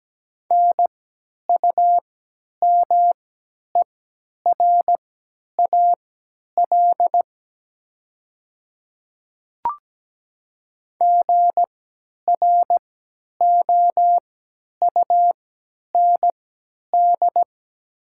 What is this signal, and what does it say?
Morse code, heard as a keyed tone, about 17 wpm, spaced as for 10 wpm, 700 Hz. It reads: NUMERAL GROUND
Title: 3x Character Spacing - Top 500 Words - 17wpm